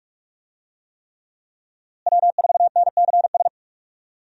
Morse code, heard as a keyed tone, about 46 wpm, 700 Hz. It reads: W4NCS
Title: USA Call Signs - 5 Characters - Repeated 3x - 45wpm